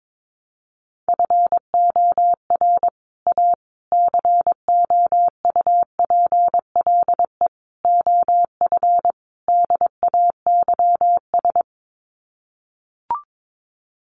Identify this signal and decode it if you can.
Morse code, heard as a keyed tone, about 22 wpm, 700 Hz. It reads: FOR A COUPLE OF DAYS
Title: Sets of 5 Words Encore2 22wpm